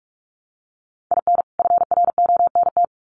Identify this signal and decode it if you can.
Morse code, heard as a keyed tone, about 44 wpm, 700 Hz. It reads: IN FRONT
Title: Sets of 2 Words 45wpm